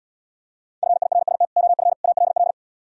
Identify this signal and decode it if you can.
Morse code, heard as a keyed tone, about 76 wpm, 700 Hz. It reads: FIRST OF ALL